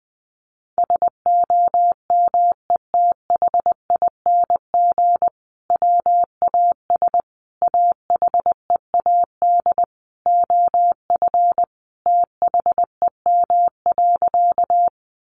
Morse code, heard as a keyed tone, about 20 wpm, 700 Hz. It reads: SOMETHING WAS AHEAD OF THEM.
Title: Sentences from Top 1100 Words 20wpm